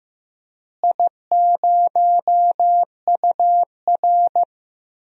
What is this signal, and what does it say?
Morse code, heard as a keyed tone, about 15 wpm, 700 Hz. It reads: I0UR
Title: DXCC-100 Call Signs - 4 Characters - Repeated 2x - 15wpm